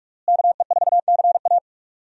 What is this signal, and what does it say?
Morse code, heard as a keyed tone, about 46 wpm, 700 Hz. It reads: KE4CA
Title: USA Call Signs - 5 Characters - Repeated 2x - 45wpm